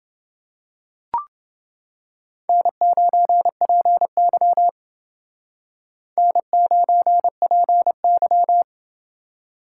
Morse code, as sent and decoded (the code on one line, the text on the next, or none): -. ----. .--. -.-- / -. ----. .--. -.--
N9PY N9PY